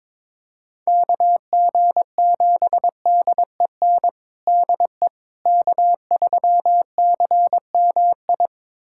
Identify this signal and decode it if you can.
Morse code, heard as a keyed tone, about 22 wpm, 700 Hz. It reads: KG7DEN DE K3CMI
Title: USA Callsign DE Callsign 22wpm